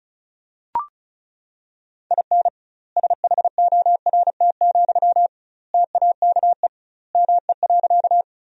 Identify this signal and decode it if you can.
Morse code, heard as a keyed tone, about 35 wpm, 700 Hz. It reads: IN SHORT, TAKE ME.
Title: 2x Word-Spacing - Sentences from Top 400 Words - 35wpm